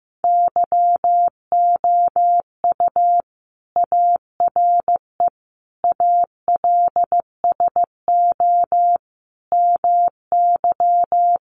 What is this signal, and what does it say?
Morse code, heard as a keyed tone, about 15 wpm, 700 Hz. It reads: YOU ARE ALSO MY
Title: Sentences from Top 1300 Words 15wpm